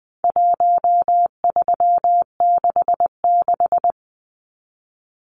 Morse code, heard as a keyed tone, about 20 wpm, 700 Hz. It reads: 1366
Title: Numbers - Contest Style - 4-Digit 20wpm